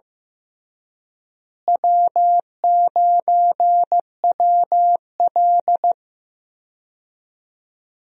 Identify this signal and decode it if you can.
Morse code, heard as a keyed tone, about 15 wpm, 700 Hz. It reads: W9WL
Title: USA Call Signs - 4 Characters - Repeated 3x - 15wpm